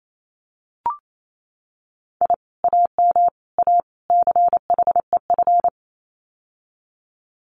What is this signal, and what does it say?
Morse code, heard as a keyed tone, about 28 wpm, 700 Hz. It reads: I AM A CHEF
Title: Common QSO Phrases Encore 28wpm